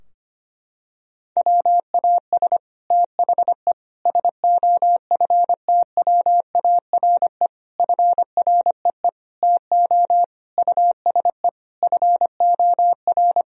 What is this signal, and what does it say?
Morse code, heard as a keyed tone, about 25 wpm, 700 Hz. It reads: WAS THE SOFTWARE FREE TO USE FOR
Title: Sentences from Top 1000 Words - Review Part 3 - 25wpm